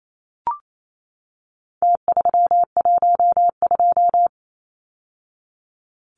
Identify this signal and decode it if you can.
Morse code, heard as a keyed tone, about 28 wpm, 700 Hz. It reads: T312
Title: Numbers - Contest Style - 4-Digit 28wpm